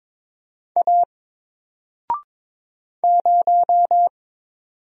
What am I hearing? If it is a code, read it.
Morse code, heard as a keyed tone, about 22 wpm, 700 Hz. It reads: A 0
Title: Single Character (Letter:Number:Punctuation:Prosign) 22wpm